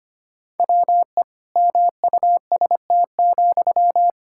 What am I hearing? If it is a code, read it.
Morse code, heard as a keyed tone, about 25 wpm, 700 Hz. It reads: WE MUST,
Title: Sentences from Top 1600 Words 25wpm